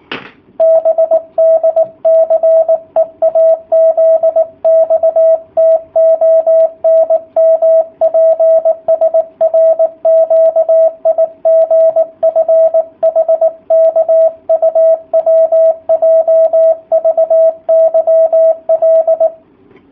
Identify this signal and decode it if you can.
Morse code, heard as a keyed tone, about 18 wpm, 650 Hz, with radio noise. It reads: BDCEAZXTONMPSRQIGFHKUWJVYL
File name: jh8cbh / 2019cw / sp18- w